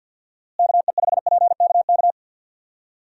Morse code, heard as a keyed tone, about 49 wpm, 700 Hz. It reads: KE5PKK
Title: USA Call Signs - 6 Characters - Repeated 2x - 50wpm